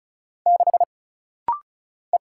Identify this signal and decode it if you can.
Morse code, heard as a keyed tone, about 35 wpm, 700 Hz. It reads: 6 E
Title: Single Letter-Number Alt - Rapid-Fire 35wpm